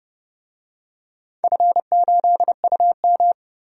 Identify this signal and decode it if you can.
Morse code, heard as a keyed tone, about 30 wpm, 700 Hz. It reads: F8UM
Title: DXCC-100 Call Signs - 4 Characters - Repeated 2x - 30wpm